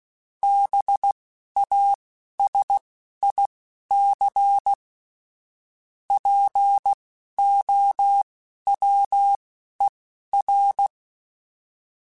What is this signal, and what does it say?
Morse code, heard as a keyed tone, about 16 wpm, spaced as for 11 wpm, 800 Hz. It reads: BASIC POWER
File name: Technical Subjects / Technical Subjects-mp3s / Sw Pwr Sup @10